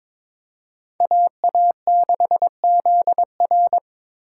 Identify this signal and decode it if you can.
Morse code, heard as a keyed tone, about 26 wpm, 690 Hz. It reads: AA6ZR
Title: Speed Racing - USA Call Signs 22wpm